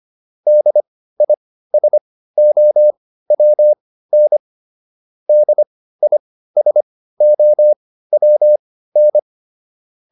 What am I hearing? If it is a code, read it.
Morse code, heard as a keyed tone, about 25 wpm, spaced as for 15 wpm, 600 Hz. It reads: DISOWN DISOWN